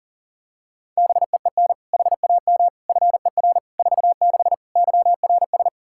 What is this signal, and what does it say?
Morse code, heard as a keyed tone, about 40 wpm, 700 Hz. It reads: BEEN HAM FER 46 YRS
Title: Practice QSO - Exchange 3 - More Info - 40wpm